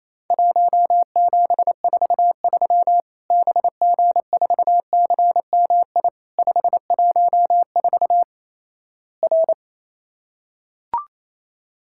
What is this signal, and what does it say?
Morse code, heard as a keyed tone, about 28 wpm, 700 Hz. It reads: 1743 BG4CMI 514 R
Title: Worked All Europe DX Contest - QTC Exchange - 28wpm